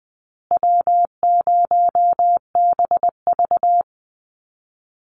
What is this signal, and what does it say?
Morse code, heard as a keyed tone, about 20 wpm, 700 Hz. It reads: W0BV
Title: Easy Call Signs Repeated 3x - 20wpm